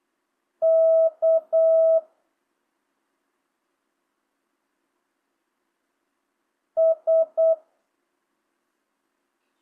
Morse code, heard as a keyed tone, about 8 wpm, 650 Hz. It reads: K S